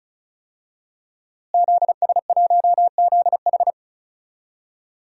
Morse code, heard as a keyed tone, about 34 wpm, 700 Hz. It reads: ZS1ZH
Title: DXCC-100 Call Signs - 5 Characters - Repeated 2x - 35wpm